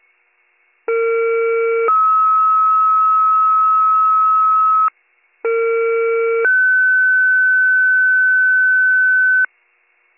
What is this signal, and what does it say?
Two-tone page, heard as a one-second tone, then a three-second tone, about 460 then 1290 Hz, about 460 then 1500 Hz. It is heard from 0.9 to 9.5 s.